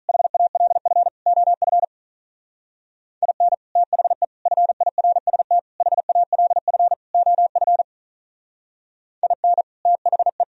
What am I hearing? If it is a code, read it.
Morse code, heard as a keyed tone, about 43 wpm, 700 Hz. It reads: HALF OF IN THE FIRST HALF OF IN THE
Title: Speed-Racing - Sets of 5 Words 35wpm